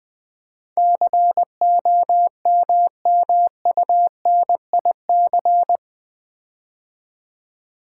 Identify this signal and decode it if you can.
Morse code, heard as a keyed tone, about 20 wpm, 700 Hz. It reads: COMMUNIC
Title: Build A Word - 4 Letters At A Time - 11-Letter Words - 20wpm